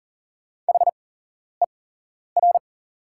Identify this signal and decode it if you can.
Morse code, heard as a keyed tone, about 40 wpm, spaced as for 11 wpm, 690 Hz. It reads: HER
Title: 8x Character Spacing - Top 100 Words - 40wpm